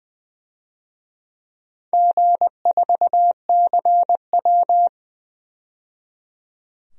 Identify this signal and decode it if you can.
Morse code, heard as a keyed tone, about 20 wpm, 700 Hz. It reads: G4CW
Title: DXCC-100 Call Signs - 4 Characters - 20wpm